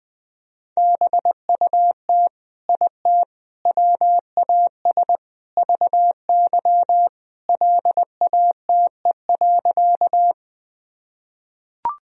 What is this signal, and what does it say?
Morse code, heard as a keyed tone, about 20 wpm, 700 Hz. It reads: BUT IT WAS VY LATE.
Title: Sentences from Top 300 Words and Abbreviations 20wpm